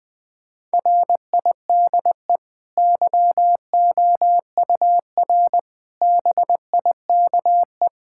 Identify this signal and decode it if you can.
Morse code, heard as a keyed tone, about 20 wpm, 700 Hz. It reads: RIDE YOUR BIKE
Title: Easy Sentences with Hints - 20wpm